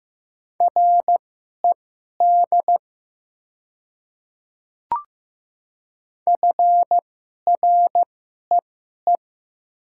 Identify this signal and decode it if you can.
Morse code, heard as a keyed tone, about 15 wpm, spaced as for 11 wpm, 700 Hz. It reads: RED FREE
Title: 2x Character Spacing - Top 500 Words - 15wpm